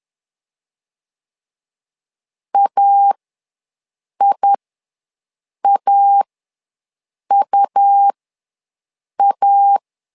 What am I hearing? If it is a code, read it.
Morse code, heard as a keyed tone, about 11 wpm, spaced as for 6 wpm, 800 Hz. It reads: AIAUA
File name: LEZIONI / 2 UHDN / 2 U ws 10wpm 2017